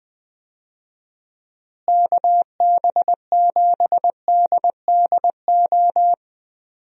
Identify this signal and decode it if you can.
Morse code, heard as a keyed tone, about 20 wpm, 700 Hz. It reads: KB7DDO
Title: USA Call Signs - 6 Characters - Repeated 2x - 20wpm